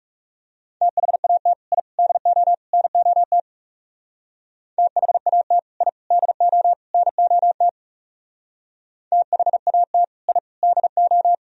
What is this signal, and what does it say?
Morse code, heard as a keyed tone, about 38 wpm, 700 Hz. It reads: THAT I DO NOT THAT I DO NOT THAT I DO